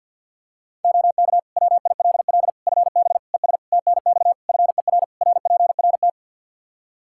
Morse code, heard as a keyed tone, about 52 wpm, 700 Hz. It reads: OK WILL FB ES TNX FER RPRT
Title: Practice QSO - Exchange 2 - Info - 50wpm